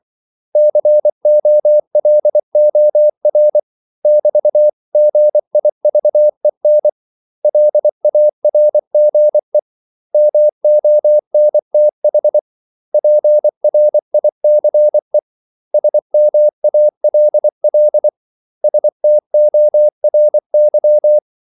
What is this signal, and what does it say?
Morse code, heard as a keyed tone, about 24 wpm, 600 Hz. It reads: COLOR <BT>GIVEN LARGE MONTH PRICE SMALL STORY
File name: CW / Practice / 500MostCommonEnglishWords24